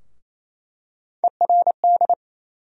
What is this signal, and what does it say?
Morse code, heard as a keyed tone, about 28 wpm, 700 Hz. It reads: ERD
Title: USA Call Sign Postfixes - 3 Letters 28wpm